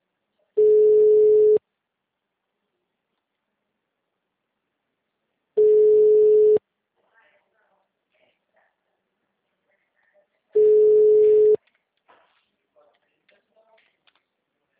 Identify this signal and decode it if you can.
Ringback tone (Europe, 425 Hz), with speech.